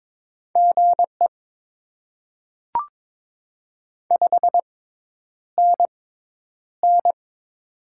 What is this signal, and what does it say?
Morse code, heard as a keyed tone, about 22 wpm, spaced as for 10 wpm, 700 Hz. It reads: GE 5NN
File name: Top 20 QSO Elements - 6x Character Spacing - 22wpm